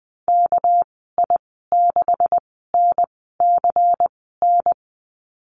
Morse code, heard as a keyed tone, about 20 wpm, spaced as for 15 wpm, 700 Hz. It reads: KI6NCN
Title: 219 Lesson 35g - Number 0 - Call Signs 20wpm